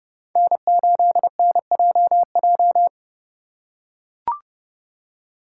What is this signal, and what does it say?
Morse code, heard as a keyed tone, about 29 wpm, 700 Hz. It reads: N8NJJ